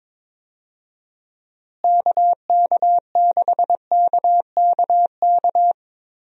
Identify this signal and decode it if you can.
Morse code, heard as a keyed tone, about 22 wpm, 700 Hz. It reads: KK6KKK
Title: USA Call Signs - 6 Characters - Repeated 3x - 22wpm